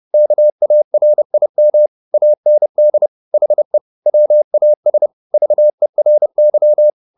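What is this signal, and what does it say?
Morse code, heard as a keyed tone, about 30 wpm, 600 Hz. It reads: KARIM AND HE WAS VERY
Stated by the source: Morse code at 30 wpm, the novel Never by Ken Follett in English, from CW med Gnister.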